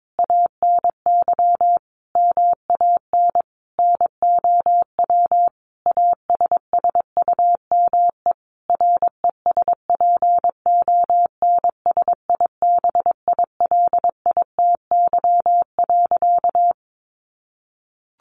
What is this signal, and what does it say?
Morse code, heard as a keyed tone, about 22 wpm, 700 Hz. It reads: ANY MAN NOW ASSUME RESPONSIBILITY.
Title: Sentences from Top 1500 Words 22wpm